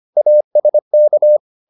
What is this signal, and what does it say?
Morse code, heard as a keyed tone, about 25 wpm, 600 Hz. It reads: ASK